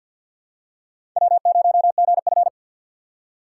Morse code, heard as a keyed tone, about 53 wpm, 700 Hz. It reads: W0GF
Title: USA Call Signs - 4x Character Spacing - 50wpm